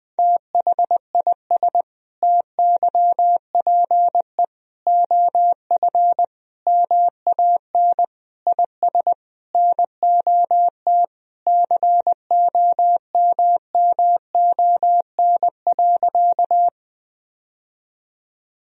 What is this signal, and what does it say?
Morse code, heard as a keyed tone, about 21 wpm, 700 Hz. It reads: THIS TYPE OF MAN IS NOT COMMON.
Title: Sentences from Top 1000 Words - Review Part 3 - 20wpm